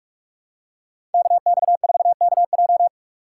Morse code, heard as a keyed tone, about 44 wpm, 700 Hz. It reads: KX4KJ
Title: USA Call Signs - 2x Character Spacing - 45wpm